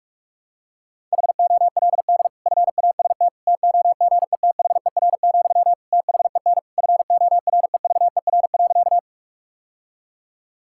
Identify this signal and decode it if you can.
Morse code, heard as a keyed tone, about 46 wpm, 700 Hz. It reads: HOLD FAST TOGETHER, THEN FOREVER.
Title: Sentences from Top 1600 Words 45wpm